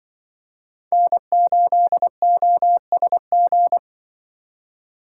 Morse code, heard as a keyed tone, about 25 wpm, 700 Hz. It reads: N8OSG
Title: Speed Racing - USA Call Signs 20wpm